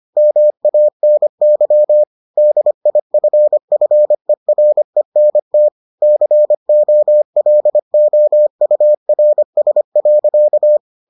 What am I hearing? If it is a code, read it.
Morse code, heard as a keyed tone, about 25 wpm, 600 Hz. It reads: MANY DIFFERENT COLOURS.